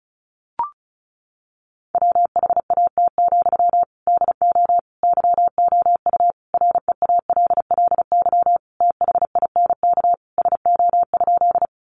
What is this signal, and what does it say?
Morse code, heard as a keyed tone, about 34 wpm, 700 Hz. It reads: WHAT, DO YOU REALLY THINK SO?
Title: Sentences from Top 600 Words 35wpm